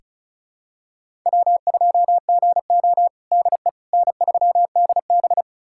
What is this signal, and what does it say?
Morse code, heard as a keyed tone, about 34 wpm, 700 Hz. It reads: W2GO DE N3DB
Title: USA Callsign DE Callsign - 4 Characters - 35wpm